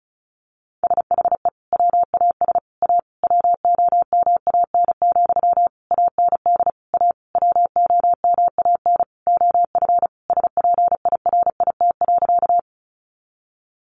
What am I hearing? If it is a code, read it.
Morse code, heard as a keyed tone, about 34 wpm, 700 Hz. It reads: SHE WAS A WOMAN, AND A WOMAN OF SPIRIT.
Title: Sentences from Top 1400 Words 35wpm